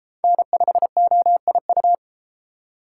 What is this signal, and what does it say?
Morse code, heard as a keyed tone, about 29 wpm, 700 Hz. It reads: N5OIU